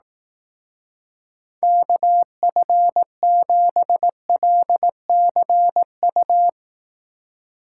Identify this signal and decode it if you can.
Morse code, heard as a keyed tone, about 18 wpm, 700 Hz. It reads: KF7LCU
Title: Speed Racing - USA Call Signs 15wpm